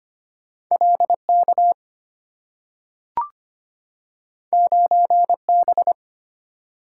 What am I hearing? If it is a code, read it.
Morse code, heard as a keyed tone, about 25 wpm, 690 Hz. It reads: LK 9B
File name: Two Letter-Number Combinations - 25wpm